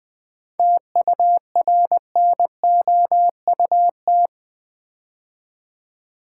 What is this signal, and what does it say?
Morse code, heard as a keyed tone, about 20 wpm, 700 Hz. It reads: TURNOUT
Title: 7 Letter Words - First 3 Letters Given - 20wpm